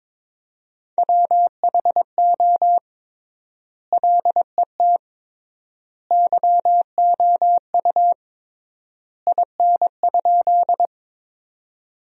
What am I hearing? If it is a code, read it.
Morse code, heard as a keyed tone, about 22 wpm, 700 Hz. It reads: WHO LET YOU IN?
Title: 3x Word-Spacing - Sentences from Top 300 Words - 22wpm